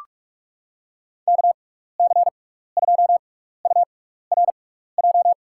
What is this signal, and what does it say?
Morse code, heard as a keyed tone, about 44 wpm, spaced as for 15 wpm, 700 Hz. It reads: KC2URJ
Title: USA Call Signs - 6 Characters - 6x Character Spacing - 45wpm